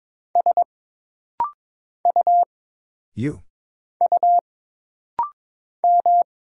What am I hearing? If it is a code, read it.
Morse code, heard as a keyed tone, about 22 wpm, 700 Hz. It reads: S U U M
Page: This rapid-fire timing is geared to push your ICR (Instant Character Recognition) skills.